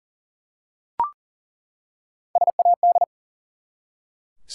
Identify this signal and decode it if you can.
Morse code, heard as a keyed tone, about 42 wpm, 700 Hz. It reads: SAD
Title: Three Letter Words - 40wpm